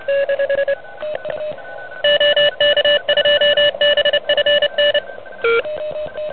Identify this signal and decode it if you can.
Morse code, heard as a keyed tone, about 30 wpm, 2950 Hz, with noise. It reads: OK2BFN
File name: truSDX / NZ contest